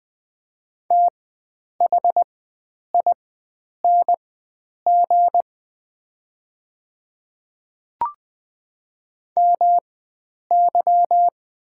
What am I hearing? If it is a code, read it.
Morse code, heard as a keyed tone, about 20 wpm, spaced as for 9 wpm, 700 Hz. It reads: THING MY
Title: 4x Character Spacing - Top 100 Words - 20wpm